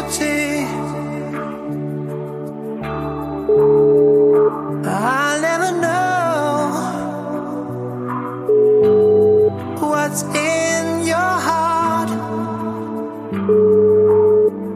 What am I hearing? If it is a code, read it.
Ringback tone (Europe, 425 Hz), with noise.